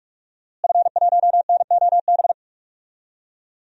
Morse code, heard as a keyed tone, about 44 wpm, 700 Hz. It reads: F1NOB
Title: DXCC-100 Call Signs - 5 Characters - Repeated 3x - 45wpm